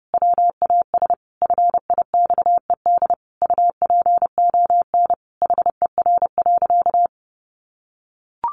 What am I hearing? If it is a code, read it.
Morse code, heard as a keyed tone, about 30 wpm, 700 Hz. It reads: WAS FIXED UPON HER.
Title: Sentences from Top 1100 Words 30wpm